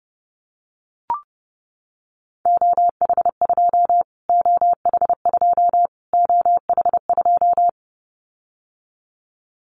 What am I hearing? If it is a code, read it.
Morse code, heard as a keyed tone, about 30 wpm, 700 Hz. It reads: OH2 OH2 OH2
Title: Callsign Prefixes for Top 25 DXCC - Repeated 3x - 30wpm